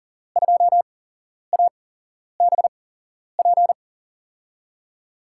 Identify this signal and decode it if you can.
Morse code, heard as a keyed tone, about 39 wpm, spaced as for 11 wpm, 700 Hz. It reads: 2ABP
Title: USA Call Signs - 6 Characters - 8x Character Spacing - 40wpm